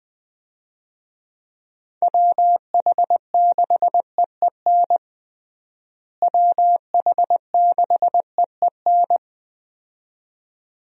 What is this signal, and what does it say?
Morse code, heard as a keyed tone, about 20 wpm, 700 Hz. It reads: WH6EEN WH6EEN
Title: USA Call Signs - 6 Characters - Repeated 2x - 20wpm